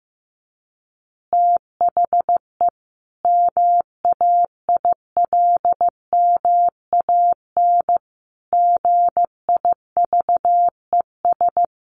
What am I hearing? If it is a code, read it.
Morse code, heard as a keyed tone, about 15 wpm, 700 Hz. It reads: THE MAILMAN GIVES
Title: Easy Phrases 15wpm